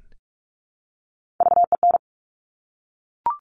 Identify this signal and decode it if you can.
Morse code, heard as a keyed tone, about 46 wpm, 700 Hz. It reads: VEN
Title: Top Three Letter Combinations in Words 45wpm